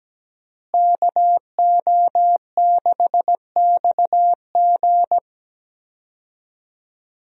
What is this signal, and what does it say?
Morse code, heard as a keyed tone, about 17 wpm, 700 Hz. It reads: KO6XG
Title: USA Call Signs - 5 Characters - Repeated 3x - 17wpm